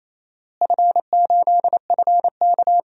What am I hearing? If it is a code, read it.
Morse code, heard as a keyed tone, about 28 wpm, 700 Hz. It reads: F8FK
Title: DXCC-100 Call Signs - 4 Characters - Repeated 2x - 28wpm